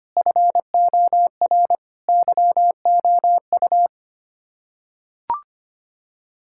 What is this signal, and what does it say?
Morse code, heard as a keyed tone, about 25 wpm, 700 Hz. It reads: FOR YOU
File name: Sets of 9 Words - 25wpm